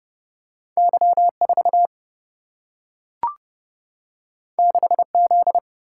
Morse code, heard as a keyed tone, about 30 wpm, 700 Hz. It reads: Y 4 6 Z
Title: Single-Letter Single-Number 30wpm